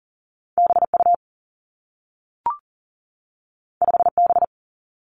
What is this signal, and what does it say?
Morse code, heard as a keyed tone, about 41 wpm, 700 Hz. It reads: BU 5B
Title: Two Challenging Letter-Number Combinations 40wpm